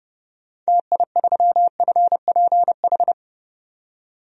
Morse code, heard as a keyed tone, about 30 wpm, 700 Hz. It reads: TI3FPH
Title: DXCC-100 Call Signs - 6 Characters - 30wpm